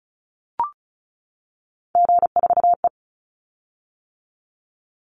Morse code, heard as a keyed tone, about 34 wpm, 700 Hz. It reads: G4E
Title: Call Signs - 35wpm